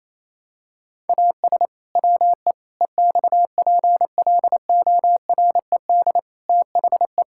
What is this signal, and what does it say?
Morse code, heard as a keyed tone, about 28 wpm, 700 Hz. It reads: AS WE EXPLORED THE
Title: Pangram Challenge v2 - 28wpm